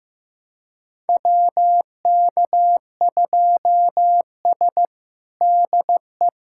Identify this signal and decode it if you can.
Morse code, heard as a keyed tone, about 16 wpm, 700 Hz. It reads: WK2S DE
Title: USA Callsign DE Callsign - 4 Characters - 15wpm